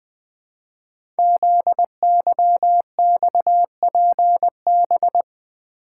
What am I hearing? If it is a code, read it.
Morse code, heard as a keyed tone, about 20 wpm, 700 Hz. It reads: ZYXPB
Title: Code Groups - 5 Letter:Numbers - 20wpm